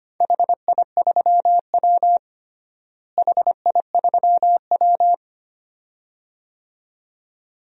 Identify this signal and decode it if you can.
Morse code, heard as a keyed tone, about 25 wpm, 700 Hz. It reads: HI3W HI3W
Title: DXCC-100 Call Signs - 4 Characters - Repeated 2x - 25wpm